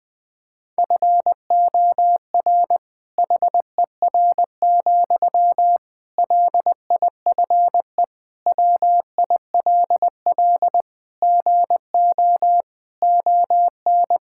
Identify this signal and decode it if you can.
Morse code, heard as a keyed tone, about 21 wpm, 700 Hz. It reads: FOR HER, LIFE WILL GO ON
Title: Sentences from Top 1600 Words 20wpm